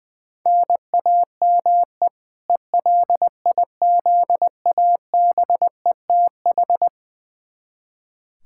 Morse code, heard as a keyed tone, about 20 wpm, 700 Hz. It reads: NAME ELIZABETH
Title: K1USN SST QSO Practice 20wpm